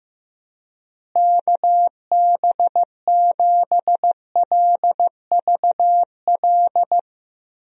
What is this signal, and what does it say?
Morse code, heard as a keyed tone, about 15 wpm, 700 Hz. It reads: KB7LVL
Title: USA Call Signs - 6 Characters - Repeated 3x - 15wpm